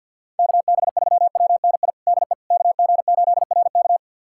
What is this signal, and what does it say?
Morse code, heard as a keyed tone, about 47 wpm, 700 Hz. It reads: KB3WNI DE KG8RK
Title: USA Callsign DE Callsign 50wpm